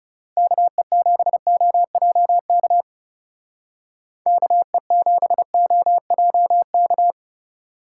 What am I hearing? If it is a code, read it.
Morse code, heard as a keyed tone, about 35 wpm, 700 Hz. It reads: KE7OJK KE7OJK
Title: Speed Racing - USA 2x3 Call Signs 30wpm